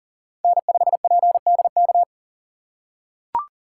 Morse code, heard as a keyed tone, about 40 wpm, 700 Hz. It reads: N5PDK
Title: USA Call Signs - 5 Characters - 3x Character Spacing - 40wpm